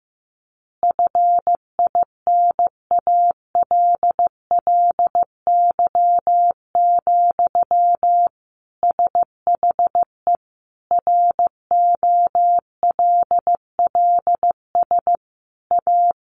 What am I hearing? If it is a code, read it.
Morse code, heard as a keyed tone, about 15 wpm, 700 Hz. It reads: FINALLY, SHE ROLLS A
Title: Easy Phrases 15wpm